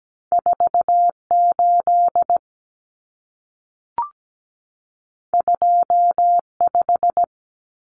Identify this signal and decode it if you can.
Morse code, heard as a keyed tone, about 17 wpm, 700 Hz. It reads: 48 25
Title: Numbers - 2-Digit 17wpm